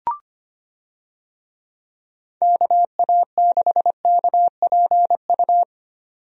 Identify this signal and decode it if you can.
Morse code, heard as a keyed tone, about 25 wpm, 700 Hz. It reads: KA6KPU
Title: USA Call Signs - 6 Characters - 25wpm